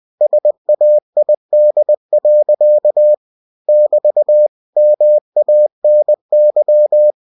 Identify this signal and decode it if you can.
Morse code, heard as a keyed tone, about 20 wpm, 600 Hz. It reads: SAID. <BT> MANY
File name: cw-books / winnie-the-pooh / winnie-20wpm-page-0054